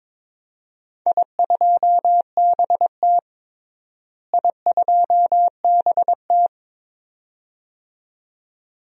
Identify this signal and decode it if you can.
Morse code, heard as a keyed tone, about 22 wpm, 700 Hz. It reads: I2BT I2BT
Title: DXCC-100 Call Signs - 4 Characters - Repeated 2x - 22wpm